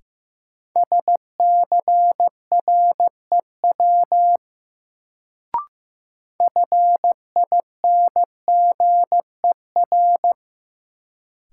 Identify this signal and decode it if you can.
Morse code, heard as a keyed tone, about 15 wpm, 700 Hz. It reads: SCREW FINGER
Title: Sentences from Top 1600 Words - New Word Drill - 15wpm